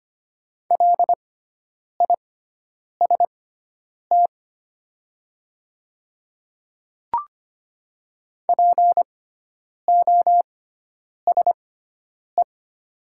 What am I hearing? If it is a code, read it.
Morse code, heard as a keyed tone, about 25 wpm, spaced as for 9 wpm, 700 Hz. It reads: LIST POSE
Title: 6x Character Spacing - Top 500 Words - 25wpm